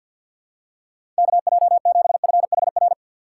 Morse code, heard as a keyed tone, about 54 wpm, 700 Hz. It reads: KJ7FHR
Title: USA Call Signs - 6 Characters - 8x Character Spacing - 50wpm